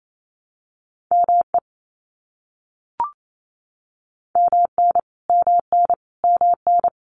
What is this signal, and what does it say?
Morse code, heard as a keyed tone, about 28 wpm, 700 Hz. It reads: ME MN MN MN
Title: US States Repeated 3x 28wpm